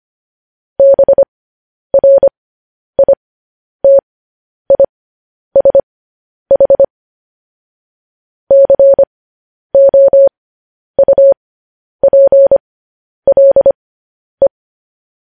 Morse code, heard as a keyed tone, about 25 wpm, spaced as for 10 wpm, 550 Hz. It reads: BRITISH COUPLE
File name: podcast / News headlines-1x-25x10